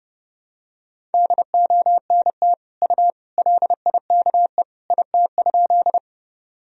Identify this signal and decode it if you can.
Morse code, heard as a keyed tone, about 30 wpm, 700 Hz. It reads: DONT U LIKE IT?
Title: Sentences from Top 300 Words and Abbreviations 30wpm